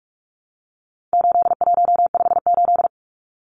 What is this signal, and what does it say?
Morse code, heard as a keyed tone, about 44 wpm, 700 Hz. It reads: 8158
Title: Numbers - 4-Digit 45wpm